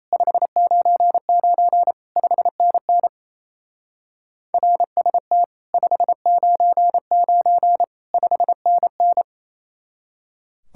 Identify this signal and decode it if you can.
Morse code, heard as a keyed tone, about 33 wpm, 700 Hz. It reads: 599 5NN RST 599 5NN
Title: Speed-Racing Common QSO Phrases Encore - 28wpm